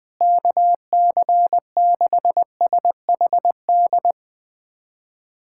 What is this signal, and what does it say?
Morse code, heard as a keyed tone, about 21 wpm, 700 Hz. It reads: KC6SHD
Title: USA Callsign DE Callsign 20wpm